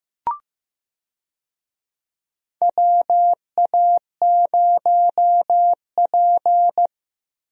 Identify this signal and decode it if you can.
Morse code, heard as a keyed tone, about 15 wpm, 700 Hz. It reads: WA0P
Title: USA Call Signs - 4 Characters - Repeated 2x - 15wpm